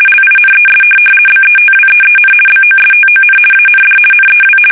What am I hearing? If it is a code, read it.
RTTY, about 50 baud. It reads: LINES BETWEEN BOSTON AND NEW
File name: fsk850-50B / teletype-corp